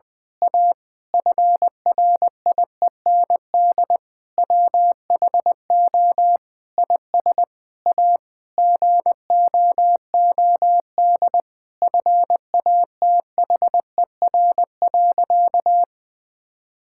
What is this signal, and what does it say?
Morse code, heard as a keyed tone, about 21 wpm, 700 Hz. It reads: A FRIEND WHO IS A GOOD FATHER.
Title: Sentences from Top 1000 Words - Review Part 5 - 20wpm